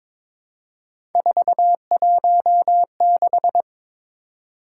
Morse code, heard as a keyed tone, about 23 wpm, 700 Hz. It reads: 416
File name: practice / 22wpm / Three Letter-Number Combinations - 22wpm